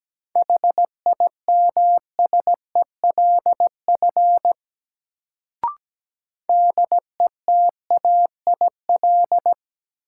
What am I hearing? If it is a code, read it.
Morse code, heard as a keyed tone, about 17 wpm, 700 Hz. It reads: HIMSELF DETAIL
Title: Sentences from Top 1500 Words - New Word Drill - 17wpm